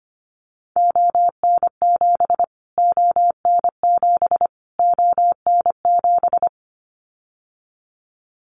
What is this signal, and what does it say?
Morse code, heard as a keyed tone, about 25 wpm, 700 Hz. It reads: ON7 ON7 ON7
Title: Callsign Prefixes for Top 25 DXCC - Repeated 3x - 25wpm